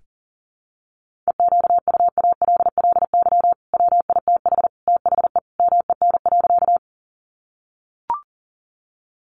Morse code, heard as a keyed tone, about 40 wpm, 700 Hz. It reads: EQUALLY WITH THE MEN.
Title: Sentences from Top 1400 Words 40wpm